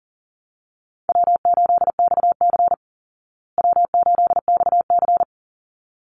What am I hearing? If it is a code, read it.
Morse code, heard as a keyed tone, about 40 wpm, 700 Hz. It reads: W8XC W8XC
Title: Easy Call Signs Repeated 2x - 40wpm